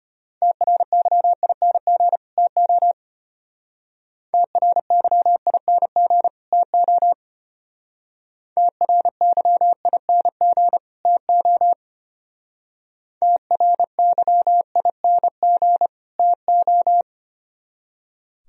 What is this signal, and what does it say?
Morse code, heard as a keyed tone, about 32 wpm, 700 Hz. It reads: TRYING TO TRYING TO TRYING TO TRYING TO